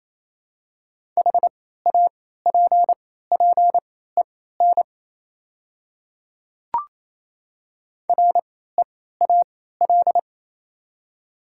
Morse code, heard as a keyed tone, about 27 wpm, spaced as for 16 wpm, 700 Hz. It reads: HAPPEN REAL
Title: 3x Character Spacing - Top 500 Words - 28wpm